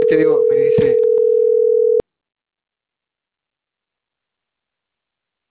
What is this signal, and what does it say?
Ringback tone (North America), 440 + 480 Hz, with speech.
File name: CENTRAL CAEA / out agcaea10